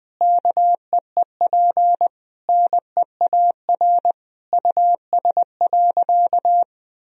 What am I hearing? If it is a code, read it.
Morse code, heard as a keyed tone, about 20 wpm, 700 Hz. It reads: KEEP NEAR US.
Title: Sentences from Top 1000 Words - Review Part 2 - 20wpm